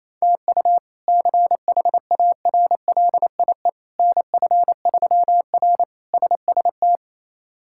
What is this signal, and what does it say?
Morse code, heard as a keyed tone, about 28 wpm, 700 Hz. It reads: TU CHARLIE NF3R SST
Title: K1USN SST QSO Practice - 28wpm